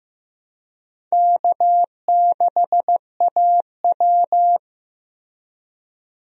Morse code, heard as a keyed tone, about 16 wpm, 700 Hz. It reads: K6AW
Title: USA Call Signs - 4 Characters - 15wpm